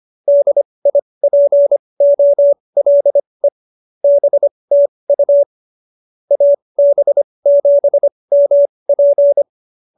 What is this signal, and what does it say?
Morse code, heard as a keyed tone, about 25 wpm, spaced as for 20 wpm, 570 Hz. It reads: DIPOLE BTU AB7MP